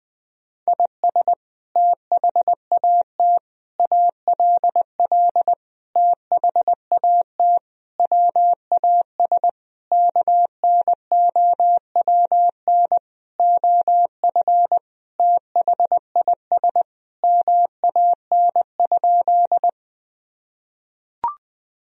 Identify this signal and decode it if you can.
Morse code, heard as a keyed tone, about 19 wpm, 700 Hz. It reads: IS THAT ALL THAT WAS KNOWN OF THIS MAN?
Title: Sentences from Top 1000 Words - Review Part 3 - 20wpm